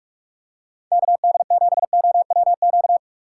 Morse code, heard as a keyed tone, about 45 wpm, 700 Hz. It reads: KD7OWQ